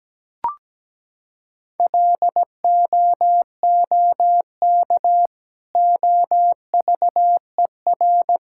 Morse code, heard as a keyed tone, about 17 wpm, 700 Hz. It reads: LOOK OVER
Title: Sentences from Top 1600 Words 17wpm